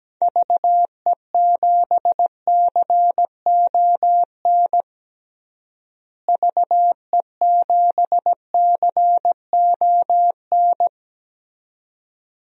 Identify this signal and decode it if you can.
Morse code, heard as a keyed tone, about 17 wpm, 700 Hz. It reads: VE7CON VE7CON
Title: DXCC-100 Call Signs - 6 Characters - Repeated 2x - 17wpm